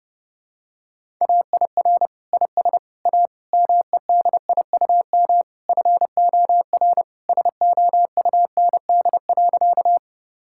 Morse code, heard as a keyed tone, about 32 wpm, 700 Hz. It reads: AIR IS A MEDIUM FOR SOUND.
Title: Sentences from Top 1000 Words - Review Part 6 - 30wpm